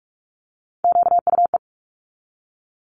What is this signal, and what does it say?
Morse code, heard as a keyed tone, about 46 wpm, 700 Hz. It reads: QUE
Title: Multi-letter Phonemes 45wpm